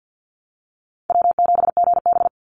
Morse code, heard as a keyed tone, about 50 wpm, 700 Hz. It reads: W7GB
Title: Call Signs - 50wpm